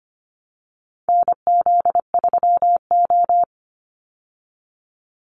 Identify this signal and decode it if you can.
Morse code, heard as a keyed tone, about 25 wpm, 700 Hz. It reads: NZ3O
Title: Call Signs Repeated 3x - 25wpm